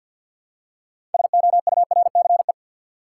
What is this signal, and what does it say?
Morse code, heard as a keyed tone, about 51 wpm, 700 Hz. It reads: SOURCE
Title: Six Letter Words - 50wpm